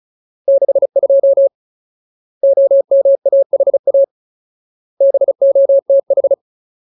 Morse code, heard as a keyed tone, about 34 wpm, 550 Hz. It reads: 62 OMAHA BOTH
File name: Ultimate Word List / 35WPM Ultimate